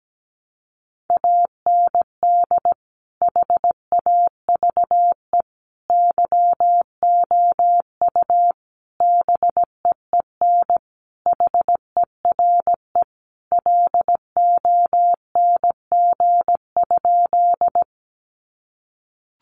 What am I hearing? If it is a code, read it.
Morse code, heard as a keyed tone, about 17 wpm, 700 Hz. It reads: AND HAVE YOU BEEN HERE LONG?
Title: Sentences from Top 200 Words 17wpm